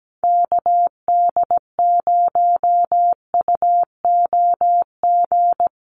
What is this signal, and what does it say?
Morse code, heard as a keyed tone, about 17 wpm, 700 Hz. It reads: KD0UOG
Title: USA Call Signs 17wpm